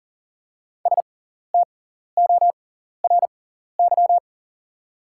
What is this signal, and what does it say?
Morse code, heard as a keyed tone, about 40 wpm, spaced as for 14 wpm, 700 Hz. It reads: STORY
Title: 6x Character Spacing - Top 300 Words - 40wpm